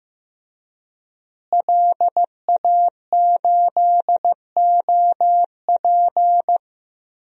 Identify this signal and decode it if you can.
Morse code, heard as a keyed tone, about 15 wpm, 700 Hz. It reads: LA8OP
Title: DXCC-100 Call Signs - 5 Characters - Repeated 3x - 15wpm